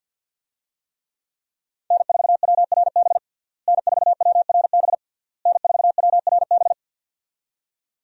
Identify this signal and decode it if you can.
Morse code, heard as a keyed tone, about 46 wpm, 700 Hz. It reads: N4WRB N4WRB N4WRB
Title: USA Call Signs - 5 Characters - Repeated 3x - 50wpm